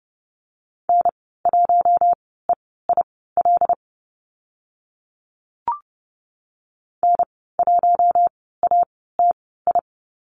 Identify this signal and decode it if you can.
Morse code, heard as a keyed tone, about 30 wpm, spaced as for 17 wpm, 700 Hz. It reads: N1EIL N1ATI
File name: Lesson 14g - Number 2 - Call Signs 30wpm